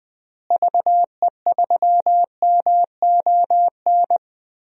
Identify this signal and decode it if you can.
Morse code, heard as a keyed tone, about 21 wpm, 700 Hz. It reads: VE3MON
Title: DXCC-100 Call Signs - 6 Characters - Repeated 3x - 20wpm